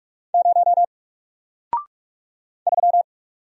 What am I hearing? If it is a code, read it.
Morse code, heard as a keyed tone, about 45 wpm, 700 Hz. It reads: 0 3
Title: Single Character (Letter:Number:Punctuation:Prosign) Alt 45wpm